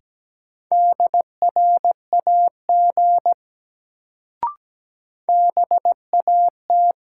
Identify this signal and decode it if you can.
Morse code, heard as a keyed tone, about 17 wpm, 700 Hz. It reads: DRAG BAT
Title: Sentences from Top 1500 Words - New Word Drill - 17wpm